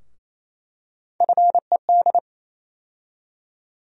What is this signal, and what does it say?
Morse code, heard as a keyed tone, about 28 wpm, 700 Hz. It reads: FED
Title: USA Call Sign Postfixes - 3 Letters 28wpm